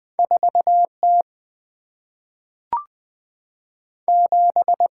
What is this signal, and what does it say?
Morse code, heard as a keyed tone, about 20 wpm, 700 Hz. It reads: 4T 7
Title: Single-Letter Single-Number 20wpm